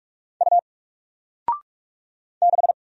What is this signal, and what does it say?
Morse code, heard as a keyed tone, about 46 wpm, 700 Hz. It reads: U 6
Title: Single Letter-Number 45wpm